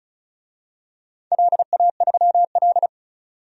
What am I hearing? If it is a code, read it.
Morse code, heard as a keyed tone, about 36 wpm, 700 Hz. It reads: LA3L
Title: DXCC-100 Call Signs - 4 Characters - 35wpm